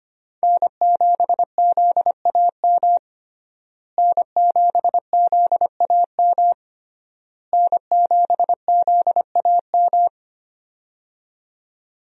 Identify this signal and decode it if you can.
Morse code, heard as a keyed tone, about 25 wpm, 700 Hz. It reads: N7ZAM N7ZAM N7ZAM
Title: USA Call Signs - 5 Characters - Repeated 3x - 25wpm